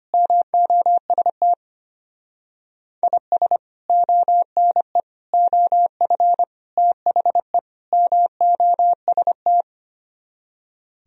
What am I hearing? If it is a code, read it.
Morse code, heard as a keyed tone, about 32 wpm, 700 Hz. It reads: MOST IS ONE OF THE MOST